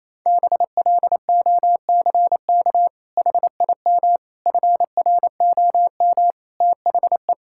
Morse code, heard as a keyed tone, about 28 wpm, 700 Hz. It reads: BLOCK HIM FROM THE
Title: Sentences from Top 1600 Words 28wpm